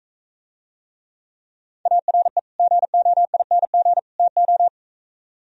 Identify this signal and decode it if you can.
Morse code, heard as a keyed tone, about 43 wpm, 700 Hz. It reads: ARE GOING TO